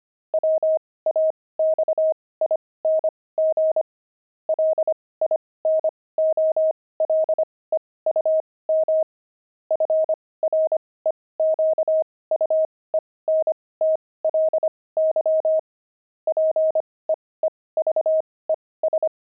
Morse code, heard as a keyed tone, about 25 wpm, spaced as for 18 wpm, 630 Hz. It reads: WAXING LINOLEUM FREQUENTLY PEEVES
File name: pangram-fs-18-cs-25-peeves